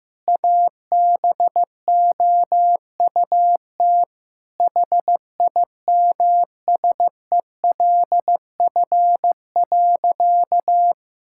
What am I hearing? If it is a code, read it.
Morse code, heard as a keyed tone, about 16 wpm, 700 Hz. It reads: ABOUT HIMSELF.
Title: Sentences from Top 1600 Words 15wpm